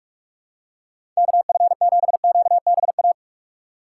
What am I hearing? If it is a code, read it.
Morse code, heard as a keyed tone, about 46 wpm, 700 Hz. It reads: KF7QBA
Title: USA Call Signs - 6 Characters - 3x Character Spacing - 45wpm